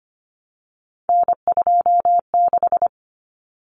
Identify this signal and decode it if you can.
Morse code, heard as a keyed tone, about 25 wpm, 700 Hz. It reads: N26
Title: Common Cut Numbers - 3-Digit 25wpm